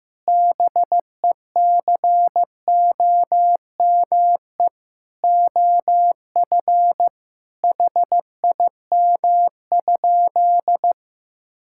Morse code, heard as a keyed tone, about 14 wpm, 700 Hz. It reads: BECOME OF HIM?
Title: Sentences from Top 1000 Words - Review Part 5 - 15wpm